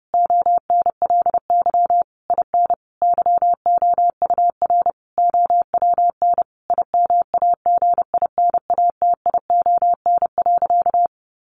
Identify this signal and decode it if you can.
Morse code, heard as a keyed tone, about 30 wpm, 700 Hz. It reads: ONLY IN YOUR OWN IMAGINATION.
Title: Sentences from Top 1300 Words 30wpm